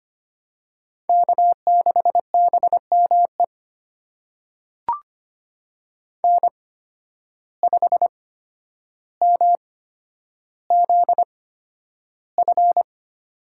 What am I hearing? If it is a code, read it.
Morse code, heard as a keyed tone, about 25 wpm, spaced as for 11 wpm, 700 Hz. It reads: K6BME N5MZF
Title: USA Call Signs - 5 Characters - 8x Character Spacing - 25wpm